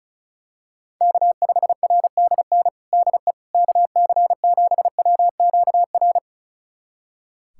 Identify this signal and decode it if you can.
Morse code, heard as a keyed tone, about 36 wpm, 700 Hz. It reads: K5RDN DE KC7WQR
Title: USA Callsign DE Callsign 35wpm